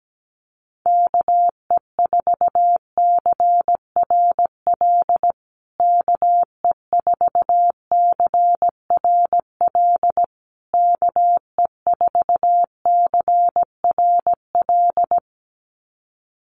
Morse code, heard as a keyed tone, about 17 wpm, 700 Hz. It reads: KE4CRL KE4CRL KE4CRL
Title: USA Call Signs Repeated 3x 17wpm